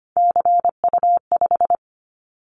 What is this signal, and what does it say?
Morse code, heard as a keyed tone, about 25 wpm, 700 Hz. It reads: CU5
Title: Callsign Prefixes for Top 50 DXCC - 25wpm